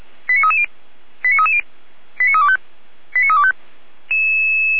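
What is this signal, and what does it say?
Five-tone selective call (ZVEI), heard as quick runs of five tones, tones of about 70 ms. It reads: 89330 89330 89325 89325
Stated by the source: Five-tone ZVEI selective call 89329 received by SDR, from the audio server of Feuerwehr Orlamünde.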